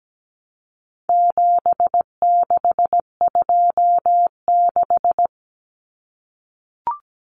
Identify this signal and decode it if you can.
Morse code, heard as a keyed tone, about 17 wpm, 700 Hz. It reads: 7626
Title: Numbers - 4-Digit 17wpm